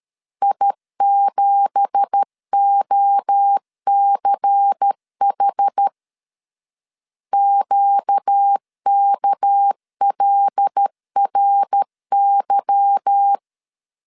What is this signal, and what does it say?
Morse code, heard as a keyed tone, about 13 wpm, 800 Hz. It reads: I7OCH QKLRY
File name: CW / DET / 1 9 1500C S B 12wpm 3819